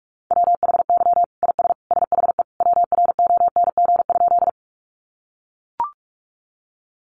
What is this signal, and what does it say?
Morse code, heard as a keyed tone, about 44 wpm, 700 Hz. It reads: WHY IS SHE WRONG?
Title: Sentences from Top 900 Words 45wpm